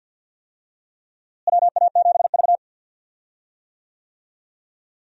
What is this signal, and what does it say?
Morse code, heard as a keyed tone, about 53 wpm, 700 Hz. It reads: WA7V
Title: USA Call Signs - 4 Characters - Repeated 2x - 50wpm